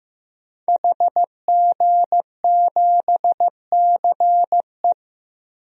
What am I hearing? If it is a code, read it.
Morse code, heard as a keyed tone, about 15 wpm, 700 Hz. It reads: HG7CE
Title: DXCC-100 Call Signs - 5 Characters - Repeated 3x - 15wpm